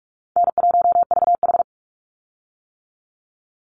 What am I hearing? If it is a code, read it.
Morse code, heard as a keyed tone, about 44 wpm, 700 Hz. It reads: N1VH
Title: Easy Call Signs - 45wpm